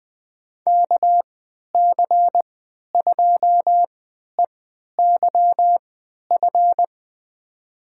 Morse code, heard as a keyed tone, about 20 wpm, spaced as for 11 wpm, 700 Hz. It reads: KC2EYF
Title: USA Call Signs - 6 Characters - 3x Character Spacing - 20wpm